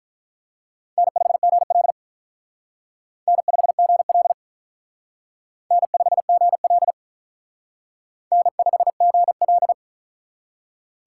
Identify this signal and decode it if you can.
Morse code, heard as a keyed tone, about 44 wpm, 700 Hz. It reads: N5GL N5GL N5GL N5GL